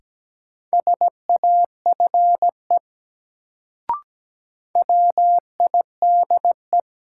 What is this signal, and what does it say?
Morse code, heard as a keyed tone, about 17 wpm, 700 Hz. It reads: SAFE WIDE
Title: Sentences from Top 900 Words - New Word Drill - 17wpm